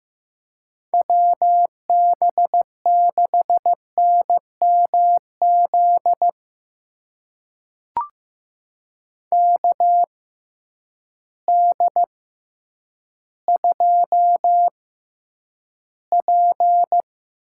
Morse code, heard as a keyed tone, about 15 wpm, spaced as for 9 wpm, 700 Hz. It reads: WB6NMZ KD2P
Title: USA Call Signs - 6 Characters - 6x Character Spacing - 15wpm